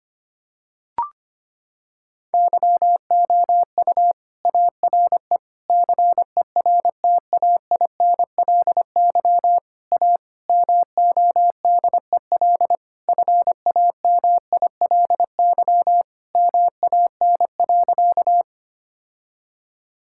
Morse code, heard as a keyed tone, about 25 wpm, 700 Hz. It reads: YOU ARE CERTAINLY A MODEL FAMILY MAN.
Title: Sentences from Top 1000 Words - Review Part 5 - 25wpm